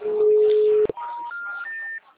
Special information tone.
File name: out- -2019- - - .36577